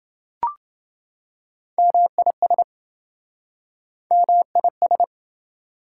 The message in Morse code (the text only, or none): MIS MIS